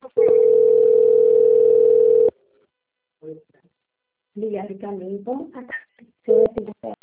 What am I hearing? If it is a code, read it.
Ringback tone (North America), 440 + 480 Hz, with speech.
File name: out- - - - - .34630